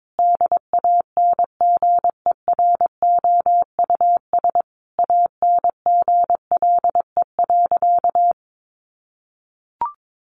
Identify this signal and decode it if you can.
Morse code, heard as a keyed tone, about 22 wpm, 700 Hz. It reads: DANGEROUS ANGLE.
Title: Sentences from Top 1500 Words 22wpm